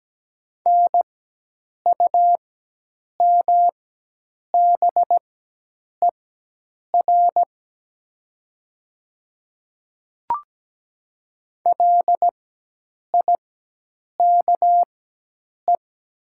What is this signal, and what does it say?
Morse code, heard as a keyed tone, about 17 wpm, spaced as for 8 wpm, 700 Hz. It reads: NUMBER LIKE
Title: 4x Character Spacing - Top 200 Words - 17wpm